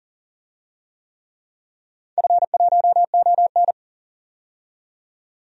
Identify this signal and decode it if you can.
Morse code, heard as a keyed tone, about 37 wpm, 700 Hz. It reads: F1ON
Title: DXCC-100 Call Signs - 4 Characters - 40wpm